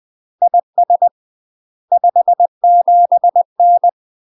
Morse code, heard as a keyed tone, about 20 wpm, 700 Hz. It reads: IS 57N
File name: UK MASTER QSOs TRAINING / 20WPM / UK MASTER QSOs 0022